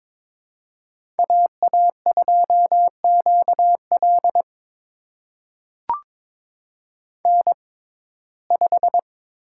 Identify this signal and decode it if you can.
Morse code, heard as a keyed tone, about 22 wpm, spaced as for 16 wpm, 700 Hz. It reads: AA2QL N5
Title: USA Call Signs - 5 Characters - 6x Character Spacing - 22wpm